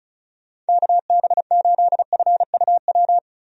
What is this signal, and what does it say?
Morse code, heard as a keyed tone, about 36 wpm, 700 Hz. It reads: KB8FUW
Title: USA Call Signs - 6 Characters - Repeated 3x - 35wpm